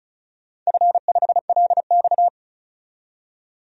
Morse code, heard as a keyed tone, about 36 wpm, 700 Hz. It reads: F5LX
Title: DXCC-100 Call Signs - 4 Characters - 35wpm